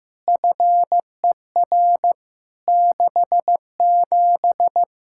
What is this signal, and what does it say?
Morse code, heard as a keyed tone, about 15 wpm, 700 Hz. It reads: FER 67
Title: Practice QSO - Exchange 3 - More Info - 15wpm